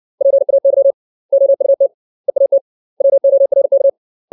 Morse code, heard as a keyed tone, about 60 wpm, 550 Hz. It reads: PAY OUT EAT WORK
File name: 632 words / 632 Words 60WPM